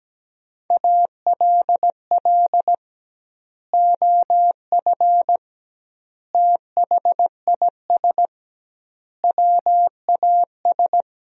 Morse code, heard as a keyed tone, about 17 wpm, 700 Hz. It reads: ALL OF THIS WAS
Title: 2x Word-Spacing - Sentences from Top 500 Words - 17wpm